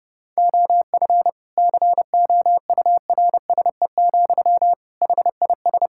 Morse code, heard as a keyed tone, about 30 wpm, 700 Hz. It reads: OF COURSE, HIS
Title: Sentences from Top 1000 Words - Review Part 5 - 30wpm